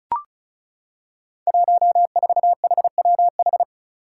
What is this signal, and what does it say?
Morse code, heard as a keyed tone, about 34 wpm, 700 Hz. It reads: 14HWH
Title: Five Challenging Letter-Number Combinations 35wpm